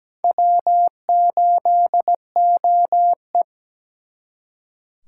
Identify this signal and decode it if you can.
Morse code, heard as a keyed tone, about 20 wpm, 700 Hz. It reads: W8OE
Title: Speed Racing - USA 1x2 Call Signs 17wpm